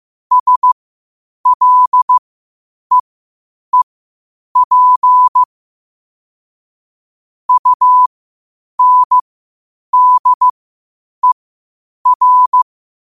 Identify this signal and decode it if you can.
Morse code, heard as a keyed tone, about 15 wpm, spaced as for 8 wpm, 1000 Hz. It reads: SLEEP UNDER
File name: qotdfiles / 8wpm